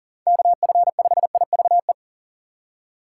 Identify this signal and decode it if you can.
Morse code, heard as a keyed tone, about 39 wpm, 700 Hz. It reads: KF5IVE
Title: USA Call Signs - 6 Characters - 6x Character Spacing - 40wpm